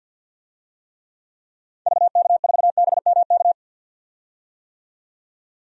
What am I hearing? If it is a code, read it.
Morse code, heard as a keyed tone, about 49 wpm, 700 Hz. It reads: VK4BMK
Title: DXCC-100 Call Signs - 6 Characters - 50wpm